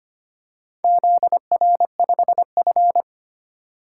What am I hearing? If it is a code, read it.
Morse code, heard as a keyed tone, about 25 wpm, 700 Hz. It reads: ZR5F
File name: practice / 25wpm / Four Letter-Number Combinations - 25wpm